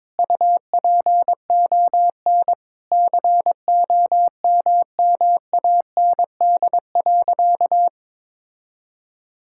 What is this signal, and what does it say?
Morse code, heard as a keyed tone, about 22 wpm, 700 Hz. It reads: UPON COMMAND.
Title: Sentences from Top 1600 Words 22wpm